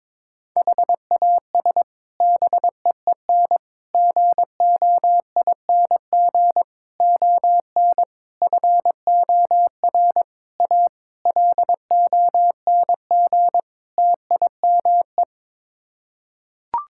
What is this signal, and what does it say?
Morse code, heard as a keyed tone, about 22 wpm, 700 Hz. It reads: HAS BEEN GOING ON FOR A LONG TIME
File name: practice / 22wpm / Sets of 8 Words - 22wpm